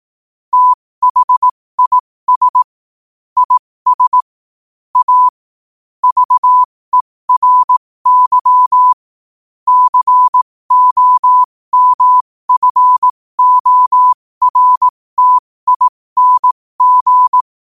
Morse code, heard as a keyed tone, about 18 wpm, spaced as for 15 wpm, 1000 Hz. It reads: THIS IS A VERY COMFORTING
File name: qotdfiles / 15wpm